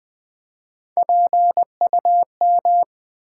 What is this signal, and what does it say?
Morse code, heard as a keyed tone, about 20 wpm, 700 Hz. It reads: PUM
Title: Build A Word - 1 Letter At A Time - 4-Letter Words - 20wpm